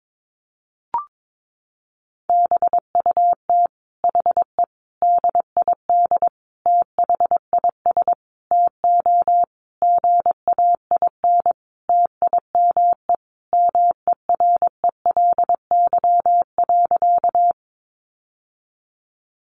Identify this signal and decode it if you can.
Morse code, heard as a keyed tone, about 22 wpm, 700 Hz. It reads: BUT HE DID THIS TO GAIN TIME MERELY.
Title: Sentences from Top 1400 Words 22wpm